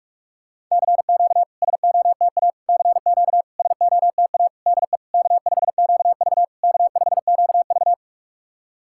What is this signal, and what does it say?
Morse code, heard as a keyed tone, about 46 wpm, 690 Hz. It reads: CQ SOTA CQ SOTA DE K5QV K5QV
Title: SOTA QSO Practice 45wpm